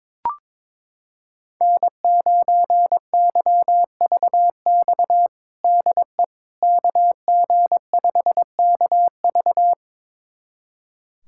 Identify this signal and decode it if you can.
Morse code, heard as a keyed tone, about 22 wpm, 700 Hz. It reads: N9YVX DE KG5KV
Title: USA Callsign DE Callsign - 5 Characters - 22wpm